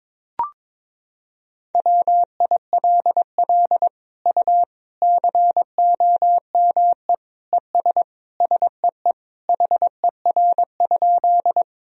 Morse code, heard as a keyed tone, about 22 wpm, 700 Hz. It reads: WILL U COME ES SEE HER?
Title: Sentences from Top 100 Words and Abbreviations 22wpm